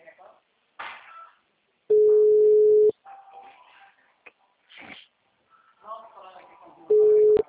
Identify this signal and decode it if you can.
Ringback tone (Europe, 425 Hz), with speech.